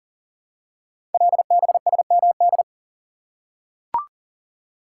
Morse code, heard as a keyed tone, about 40 wpm, 700 Hz. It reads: LBSMD
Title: Code Groups - 5 Letters - 40wpm